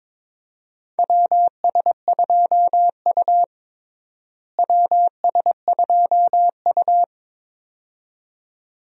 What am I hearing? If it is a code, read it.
Morse code, heard as a keyed tone, about 21 wpm, 700 Hz. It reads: WS2U WS2U
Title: USA Call Signs - 4 Characters - Repeated 2x - 22wpm